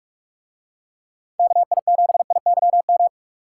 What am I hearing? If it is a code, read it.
Morse code, heard as a keyed tone, about 41 wpm, 700 Hz. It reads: KI7IYM